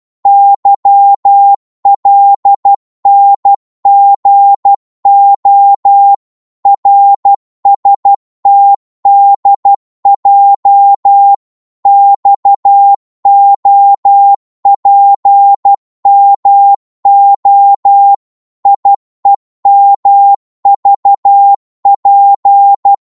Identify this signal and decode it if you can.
Morse code, heard as a keyed tone, about 12 wpm, 800 Hz. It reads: YLNGO RSTDJ XOPMO IEMVP
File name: Morse / ltr-12wpm